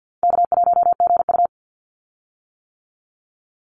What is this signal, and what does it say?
Morse code, heard as a keyed tone, about 50 wpm, 700 Hz. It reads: K1GU
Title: Call Signs - 50wpm